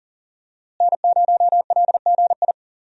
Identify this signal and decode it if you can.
Morse code, heard as a keyed tone, about 40 wpm, 700 Hz. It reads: N0LGI
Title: USA Call Signs - 5 Characters - Repeated 2x - 40wpm